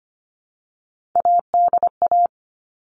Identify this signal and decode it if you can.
Morse code, heard as a keyed tone, about 25 wpm, 700 Hz. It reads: ADA
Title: All Cut Numbers - 3-Digit 25wpm